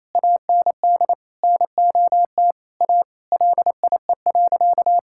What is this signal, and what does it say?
Morse code, heard as a keyed tone, about 28 wpm, 700 Hz. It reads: AND NOT A LIE.
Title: Sentences from Top 1600 Words 28wpm